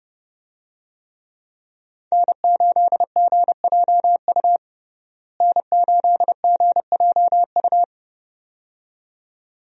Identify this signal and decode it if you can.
Morse code, heard as a keyed tone, about 30 wpm, 700 Hz. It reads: N8GJU N8GJU
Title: USA Call Signs - 5 Characters - Repeated 2x - 30wpm